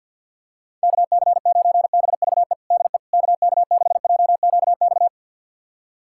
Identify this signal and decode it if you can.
Morse code, heard as a keyed tone, about 52 wpm, 700 Hz. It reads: KC9BVE DE KK6JQX
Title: USA Callsign DE Callsign - 6 Characters - 50wpm